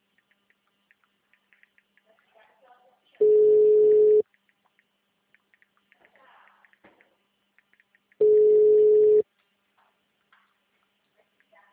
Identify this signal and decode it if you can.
Ringback tone (Europe, 425 Hz), with speech.
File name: out- -2023